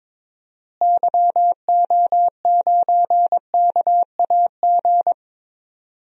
Morse code, heard as a keyed tone, about 21 wpm, 700 Hz. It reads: YO9KAG
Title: DXCC-100 Call Signs - 6 Characters - Repeated 3x - 22wpm